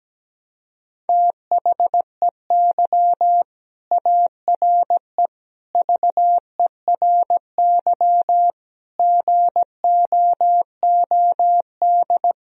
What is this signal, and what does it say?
Morse code, heard as a keyed tone, about 17 wpm, 700 Hz. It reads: THEY ARE VERY GOOD
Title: Sentences from Top 1000 Words - Review Part 3 - 17wpm